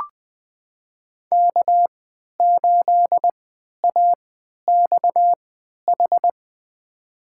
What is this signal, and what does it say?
Morse code, heard as a keyed tone, about 20 wpm, spaced as for 11 wpm, 700 Hz. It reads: K8AXH
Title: USA Call Signs - 5 Characters - 3x Character Spacing - 20wpm